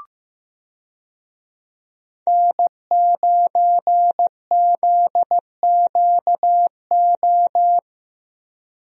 Morse code, heard as a keyed tone, about 16 wpm, 700 Hz. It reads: N9ZQO
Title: USA Call Signs - 5 Characters - Repeated 2x - 15wpm